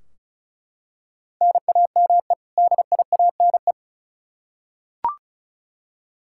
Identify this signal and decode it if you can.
Morse code, heard as a keyed tone, about 36 wpm, 700 Hz. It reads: NAME DIANE
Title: K1USN SST QSO Practice 35wpm